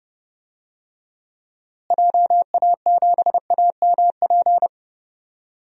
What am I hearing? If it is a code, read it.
Morse code, heard as a keyed tone, about 30 wpm, 700 Hz. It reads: JA7AMP
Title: DXCC-100 Call Signs - 6 Characters - Repeated 3x - 30wpm